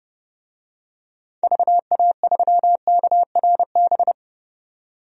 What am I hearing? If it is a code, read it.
Morse code, heard as a keyed tone, about 30 wpm, 700 Hz. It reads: VA3KRB
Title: DXCC-100 Call Signs - 6 Characters - 30wpm